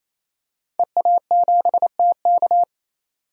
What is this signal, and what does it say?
Morse code, heard as a keyed tone, about 27 wpm, 700 Hz. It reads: EA7TK
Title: DXCC-100 Call Signs - 5 Characters - 28wpm